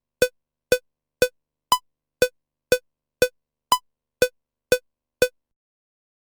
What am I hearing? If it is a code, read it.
There is no code.